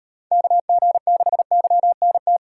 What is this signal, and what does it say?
Morse code, heard as a keyed tone, about 35 wpm, 700 Hz. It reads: KG6YNT